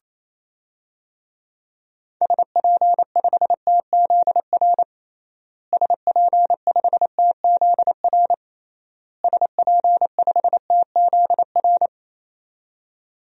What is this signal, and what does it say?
Morse code, heard as a keyed tone, about 27 wpm, 700 Hz. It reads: SP5TZR SP5TZR SP5TZR
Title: DXCC-100 Call Signs - 6 Characters - Repeated 3x - 28wpm